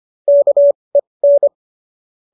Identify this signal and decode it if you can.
Morse code, heard as a keyed tone, about 25 wpm, spaced as for 20 wpm, 590 Hz. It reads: KEN
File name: qso205 20